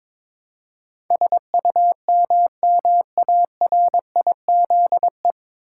Morse code, heard as a keyed tone, about 22 wpm, 700 Hz. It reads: SUMMARIZE
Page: Morse Code Ninja practice 22wpm